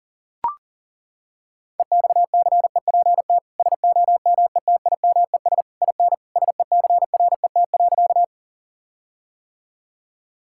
Morse code, heard as a keyed tone, about 37 wpm, 700 Hz. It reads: EXCEPT SOMETIMES IN SECRET.
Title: Sentences from Top 1600 Words 40wpm